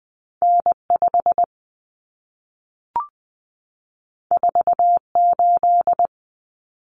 Morse code, heard as a keyed tone, about 20 wpm, 700 Hz. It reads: N5 48
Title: Common Cut Numbers - 2-Digit 20wpm